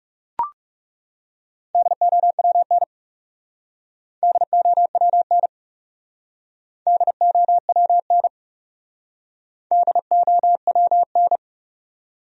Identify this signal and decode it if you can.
Morse code, heard as a keyed tone, about 39 wpm, 700 Hz. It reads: DOWN DOWN DOWN DOWN